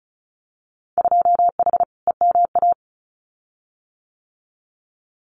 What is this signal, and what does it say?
Morse code, heard as a keyed tone, about 36 wpm, 690 Hz. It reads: 2H EMA
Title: Winter Field Day Exchanges 35wpm